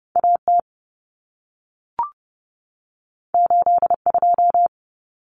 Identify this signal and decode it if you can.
Morse code, heard as a keyed tone, about 30 wpm, 700 Hz. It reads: AT 82
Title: Common Cut Numbers - 2-Digit 30wpm